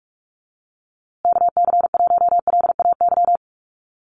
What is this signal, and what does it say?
Morse code, heard as a keyed tone, about 44 wpm, 700 Hz. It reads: KC1LAY
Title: USA Call Signs Repeated 2x 45wpm